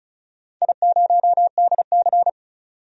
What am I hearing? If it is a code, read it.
Morse code, heard as a keyed tone, about 36 wpm, 700 Hz. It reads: I0DC
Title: DXCC-100 Call Signs - 4 Characters - 35wpm